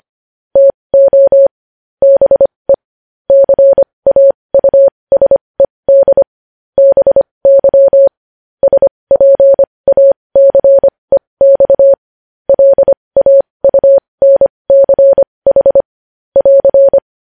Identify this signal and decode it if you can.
Morse code, heard as a keyed tone, about 25 wpm, spaced as for 20 wpm, 550 Hz. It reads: TO BE CAUSED BY SPACEX LAUNCH <AR>
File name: podcast / News headlines-1x-25x20